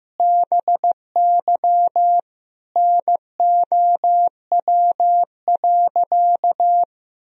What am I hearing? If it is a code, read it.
Morse code, heard as a keyed tone, about 15 wpm, 700 Hz. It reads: BY NOW.
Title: Sentences from Top 1000 Words - Review Part 3 - 15wpm